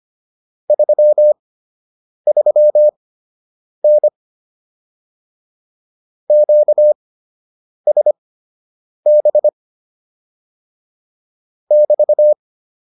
Morse code, heard as a keyed tone, about 25 wpm, spaced as for 8 wpm, 600 Hz. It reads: 33N QSB <BT>